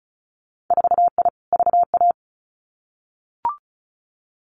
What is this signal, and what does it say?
Morse code, heard as a keyed tone, about 36 wpm, 700 Hz. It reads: 4I VA
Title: Winter Field Day Exchanges 35wpm